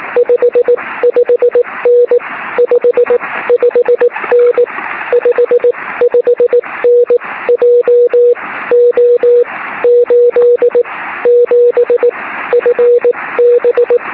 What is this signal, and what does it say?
Morse code, heard as a keyed tone, about 19 wpm, 480 Hz. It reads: 5 5N 5 5N 5 5N J O 8 7 F B
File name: sm7lcb / 08-sm7sjr 23cm